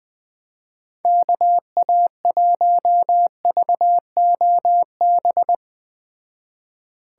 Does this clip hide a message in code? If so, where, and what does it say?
Yes: Morse code from 1.1 to 5.5 s, heard as a keyed tone, about 20 wpm, 700 Hz. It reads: KA1VOB